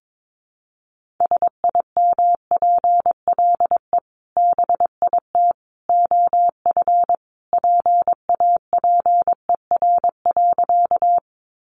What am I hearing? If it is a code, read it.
Morse code, heard as a keyed tone, about 22 wpm, 700 Hz. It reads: SIMPLE BIT OF PAPER.
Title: Sentences from Top 900 Words 22wpm